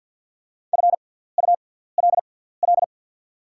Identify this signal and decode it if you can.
Morse code, heard as a keyed tone, about 51 wpm, spaced as for 17 wpm, 690 Hz. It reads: FULL
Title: 6x Character Spacing - Top 500 Words - 50wpm